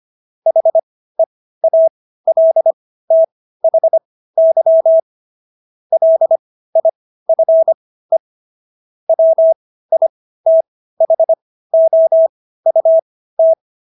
Morse code, heard as a keyed tone, about 25 wpm, spaced as for 15 wpm, 650 Hz. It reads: HEALTHY LIFE WITHOUT